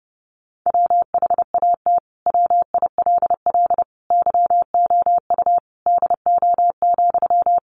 Morse code, heard as a keyed tone, about 30 wpm, 700 Hz. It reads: WHAT WILL YOU DO,
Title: Sentences from Top 100 Words 30wpm